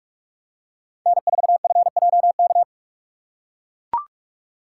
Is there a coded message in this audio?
Yes: Morse code from 1.1 to 2.6 s, heard as a keyed tone, about 44 wpm, 700 Hz. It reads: N4FJK